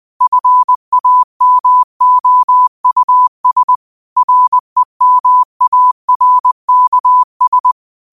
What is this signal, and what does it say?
Morse code, heard as a keyed tone, about 20 wpm, 1000 Hz. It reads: FAMOUS REMARKS